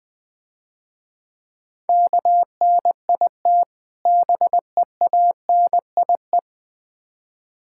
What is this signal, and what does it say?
Morse code, heard as a keyed tone, about 23 wpm, 700 Hz. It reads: KNIT BEANIE
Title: Difficult Words - Speed-Racing with Hints - 20wpm